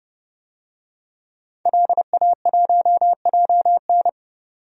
Morse code, heard as a keyed tone, about 31 wpm, 700 Hz. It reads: LA1JN
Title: DXCC-100 Call Signs - 5 Characters - 30wpm